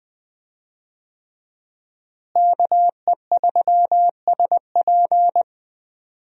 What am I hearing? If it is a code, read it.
Morse code, heard as a keyed tone, about 20 wpm, 700 Hz. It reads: KE3SP
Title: USA Call Signs - 5 Characters - Repeated 3x - 20wpm